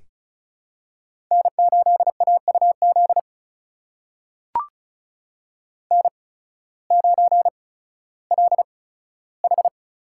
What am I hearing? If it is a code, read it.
Morse code, heard as a keyed tone, about 36 wpm, spaced as for 16 wpm, 700 Hz. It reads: N8AUZ N9LH
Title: USA Call Signs - 5 Characters - 8x Character Spacing - 35wpm